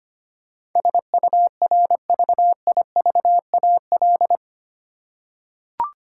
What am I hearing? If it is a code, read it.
Morse code, heard as a keyed tone, about 25 wpm, 700 Hz. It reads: SURVIVAL
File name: Eight Letter Words - 25wpm